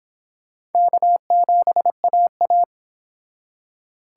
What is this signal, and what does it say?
Morse code, heard as a keyed tone, about 30 wpm, 700 Hz. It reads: K7AA